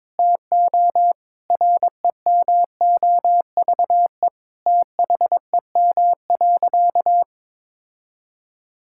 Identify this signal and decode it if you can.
Morse code, heard as a keyed tone, about 22 wpm, 700 Hz. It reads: TO REMOVE THEM.
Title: Sentences from Top 1000 Words - Review Part 5 - 22wpm